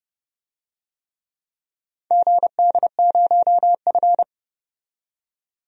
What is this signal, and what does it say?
Morse code, heard as a keyed tone, about 30 wpm, 700 Hz. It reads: GD0F
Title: DXCC-100 Call Signs - 4 Characters - 30wpm